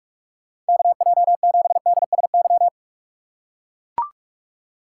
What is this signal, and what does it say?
Morse code, heard as a keyed tone, about 44 wpm, 700 Hz. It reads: KJ7DSY
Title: USA Call Signs - 6 Characters - 6x Character Spacing - 45wpm